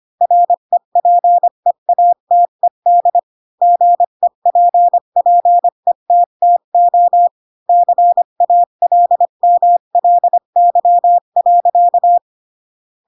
Morse code, heard as a keyed tone, about 25 wpm, 700 Hz. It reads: REPEATED GEPPETTO CALMLY.
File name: Pinocchio / Pinocchio ch7 20 25 ik1hge